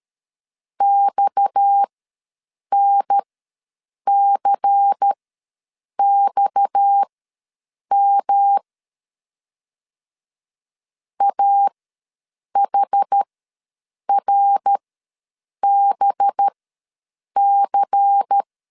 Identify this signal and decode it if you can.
Morse code, heard as a keyed tone, about 13 wpm, spaced as for 7 wpm, 800 Hz. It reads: XNCXM AHRBC
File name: LEZIONI / 6 PKX0 / 6 PKX ws 12wpm 3106